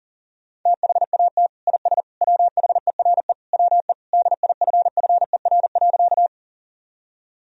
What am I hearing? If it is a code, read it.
Morse code, heard as a keyed tone, about 40 wpm, 700 Hz. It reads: THAT IS WHERE WE DIFFER.
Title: Sentences from Top 1000 Words - Review Part 1 - 40wpm